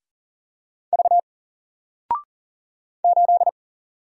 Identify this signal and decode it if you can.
Morse code, heard as a keyed tone, about 40 wpm, 700 Hz. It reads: V 8
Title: Single Letter-Number 40wpm